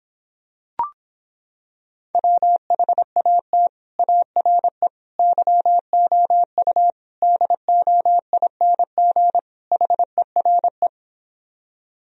Morse code, heard as a keyed tone, about 26 wpm, 700 Hz. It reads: WHAT ARE YOU DOING HERE